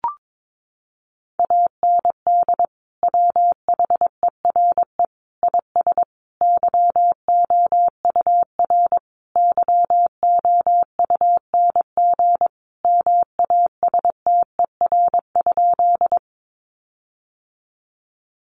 Morse code, heard as a keyed tone, about 22 wpm, 700 Hz. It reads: AND WHERE IS YOUR YOUNG MASTER?
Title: Sentences from Top 1300 Words 22wpm